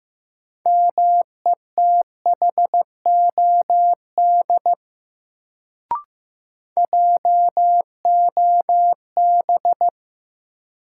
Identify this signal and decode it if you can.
Morse code, heard as a keyed tone, about 15 wpm, 700 Hz. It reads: METHOD JOB
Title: Sentences from Top 600 Words - New Word Drill - 15wpm